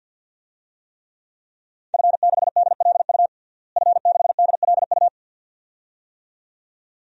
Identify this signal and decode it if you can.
Morse code, heard as a keyed tone, about 52 wpm, 700 Hz. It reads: F6DLU F6DLU
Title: DXCC-100 Call Signs - 5 Characters - Repeated 2x - 50wpm